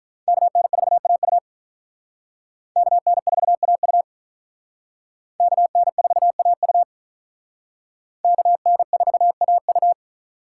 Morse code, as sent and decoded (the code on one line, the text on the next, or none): -.- -. ....- .- ..- / -.- -. ....- .- ..- / -.- -. ....- .- ..- / -.- -. ....- .- ..-
KN4AU KN4AU KN4AU KN4AU